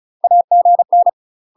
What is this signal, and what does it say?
Morse code, heard as a keyed tone, about 36 wpm, 700 Hz. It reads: AGN
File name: UK MASTER QSOs TRAINING / 35WPM / UK MASTER QSOs 0011